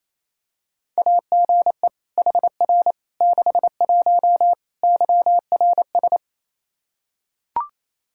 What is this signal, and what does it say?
Morse code, heard as a keyed tone, about 28 wpm, 700 Hz. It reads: AGE HR 61 YRS
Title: Practice QSO - Exchange 3 - More Info - 28wpm